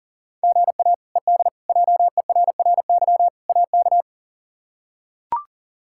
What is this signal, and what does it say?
Morse code, heard as a keyed tone, about 40 wpm, 700 Hz. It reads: GA ED JERRY AK
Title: K1USN SST QSO Practice - 40wpm